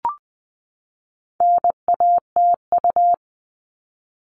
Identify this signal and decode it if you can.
Morse code, heard as a keyed tone, about 20 wpm, 700 Hz. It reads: NATU
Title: All Cut Numbers - 4-Digit 20wpm